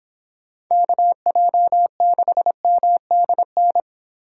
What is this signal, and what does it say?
Morse code, heard as a keyed tone, about 24 wpm, 700 Hz. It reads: KJ6MDN